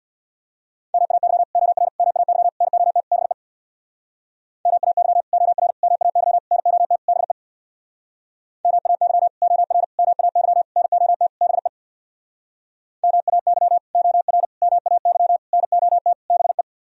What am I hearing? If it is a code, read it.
Morse code, heard as a keyed tone, about 64 wpm, 700 Hz. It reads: MAY OR MAY NOT BE MAY OR MAY NOT BE MAY OR MAY NOT BE MAY OR MAY NOT BE